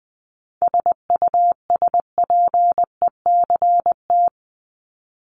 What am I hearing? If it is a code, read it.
Morse code, heard as a keyed tone, about 20 wpm, 700 Hz. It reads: SUSPECT
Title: Top 1751-2000 Words 20wpm